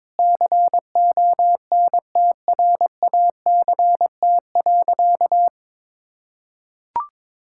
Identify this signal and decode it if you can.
Morse code, heard as a keyed tone, about 22 wpm, 700 Hz. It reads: CONTRACT.
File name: Sentences with 9-Word Phrases - 22wpm